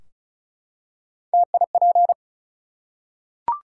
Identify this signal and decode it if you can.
Morse code, heard as a keyed tone, about 35 wpm, 700 Hz. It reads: TIP
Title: Three Letter Words - 35wpm